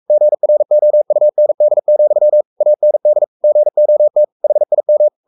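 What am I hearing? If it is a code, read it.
Morse code, heard as a keyed tone, about 44 wpm, 600 Hz. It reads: GROUND, AND GOT HIM
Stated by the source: Morse code at 43 wpm sending Ken Follett's novel Never in English, from CW med Gnister.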